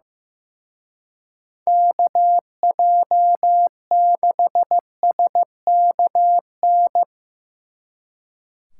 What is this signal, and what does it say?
Morse code, heard as a keyed tone, about 16 wpm, 700 Hz. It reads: KJ6SKN
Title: Speed Racing - USA 2x3 Call Signs 15wpm